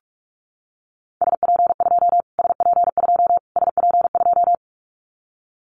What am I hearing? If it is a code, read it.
Morse code, heard as a keyed tone, about 44 wpm, 700 Hz. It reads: SP2 SP2 SP2
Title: Callsign Prefixes for Top 25 DXCC - Repeated 3x - 45wpm